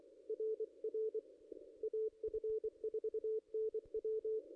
Morse code, heard as a keyed tone, about 23 wpm, 440 Hz. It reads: RR AF4NW
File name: contests / ss-cw-k5tr-op-n5rz -0600